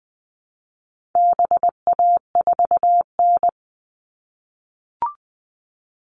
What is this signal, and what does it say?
Morse code, heard as a keyed tone, about 20 wpm, 700 Hz. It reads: BA4N
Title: All Cut Numbers - 4-Digit 20wpm